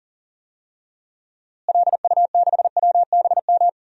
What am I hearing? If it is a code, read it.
Morse code, heard as a keyed tone, about 40 wpm, 700 Hz. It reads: LU6WBM
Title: DXCC-100 Call Signs - 6 Characters - Repeated 2x - 40wpm